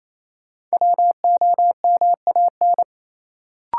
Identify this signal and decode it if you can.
Morse code, heard as a keyed tone, about 28 wpm, 700 Hz. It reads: WOMAN
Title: Sentences from Top 1000 Words - Review Part 2 - New Word Drill - 28wpm